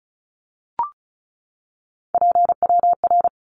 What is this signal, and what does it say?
Morse code, heard as a keyed tone, about 34 wpm, 700 Hz. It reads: PWR
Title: Abbreviations 35wpm